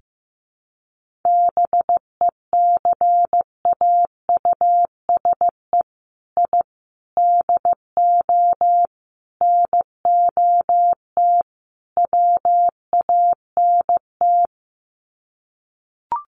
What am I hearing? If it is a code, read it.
Morse code, heard as a keyed tone, about 15 wpm, 700 Hz. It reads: BECAUSE I DO NOT WANT
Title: Sets of 5 Words Encore 15wpm